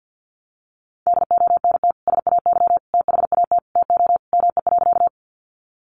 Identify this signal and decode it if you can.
Morse code, heard as a keyed tone, about 50 wpm, 700 Hz. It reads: DONT SAY THAT TO ME.
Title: Sentences from Top 300 Words 50wpm